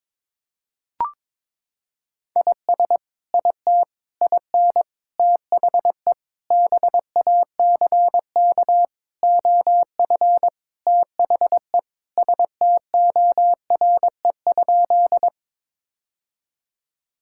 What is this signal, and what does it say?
Morse code, heard as a keyed tone, about 22 wpm, 700 Hz. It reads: IS IT IN THE BACK OF THE STORE?
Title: Sentences from Top 1000 Words - Review Part 5 - 22wpm